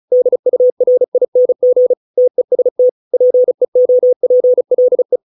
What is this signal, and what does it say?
Morse code, heard as a keyed tone, about 34 wpm, 500 Hz. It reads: DURING TEST PEOPLE